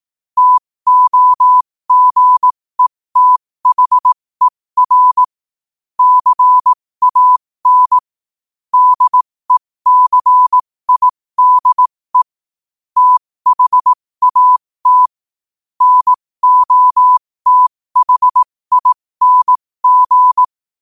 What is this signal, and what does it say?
Morse code, heard as a keyed tone, about 18 wpm, spaced as for 15 wpm, 1000 Hz. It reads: TOGETHER CAN DECIDE THAT NOTHING
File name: qotdfiles / 15wpm